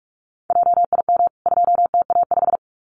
Morse code, heard as a keyed tone, about 44 wpm, 700 Hz. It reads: JIM 2TA5
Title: CWT Exchange 45wpm